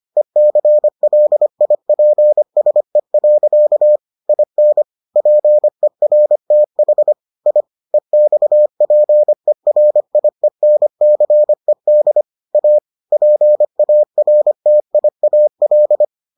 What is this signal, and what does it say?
Morse code, heard as a keyed tone, about 25 wpm, 600 Hz. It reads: ECLIPSE. IN PERTH I EXPERIENCED A PARTIAL